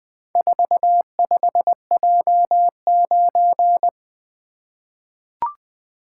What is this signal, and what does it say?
Morse code, heard as a keyed tone, about 20 wpm, 700 Hz. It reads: 45J9
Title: Four Letter-Number Combinations - 20wpm